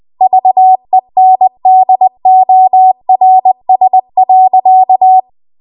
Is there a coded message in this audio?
Yes: Morse code from 0.2 to 5.2 s, heard as a keyed tone, about 20 wpm, 750 Hz. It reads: VENDORS.